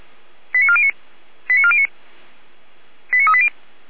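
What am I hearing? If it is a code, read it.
Five-tone selective call (ZVEI), heard as quick runs of five tones, tones of about 70 ms. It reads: 89409 89409 89380
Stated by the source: ZVEI five-tone selective call received on an SDR.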